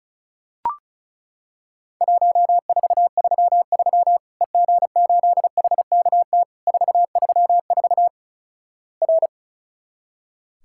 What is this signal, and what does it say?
Morse code, heard as a keyed tone, about 36 wpm, 700 Hz. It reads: 1433 EG8HKT 434 R
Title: Worked All Europe DX Contest - QTC Exchange - 35wpm